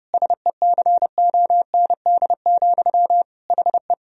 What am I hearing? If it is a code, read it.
Morse code, heard as a keyed tone, about 32 wpm, 700 Hz. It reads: SECOND, HE
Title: Sentences from Top 1000 Words - Review Part 1 - 30wpm